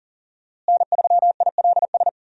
Morse code, heard as a keyed tone, about 40 wpm, 700 Hz. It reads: N3ILS
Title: USA Call Signs - 5 Characters - Repeated 2x - 40wpm